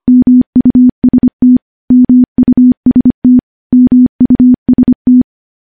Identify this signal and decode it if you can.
Morse code, heard as a keyed tone, about 25 wpm, 250 Hz. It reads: MUST MUST MUST
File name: morse / morse-training-25WPM